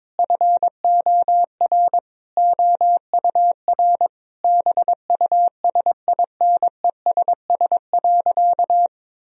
Morse code, heard as a keyed tone, about 22 wpm, 700 Hz. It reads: FOR OUR BUSINESS.
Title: Sentences from Top 1000 Words - Review Part 4 - 22wpm